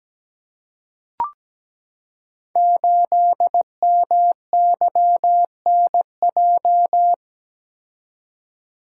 Morse code, heard as a keyed tone, about 17 wpm, 700 Hz. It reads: 8MYNJ
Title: Code Groups - 5 Letter:Numbers - 17wpm